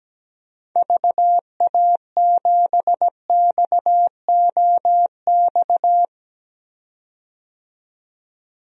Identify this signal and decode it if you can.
Morse code, heard as a keyed tone, about 17 wpm, 700 Hz. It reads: VA7XOX
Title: DXCC-100 Call Signs - 6 Characters - 17wpm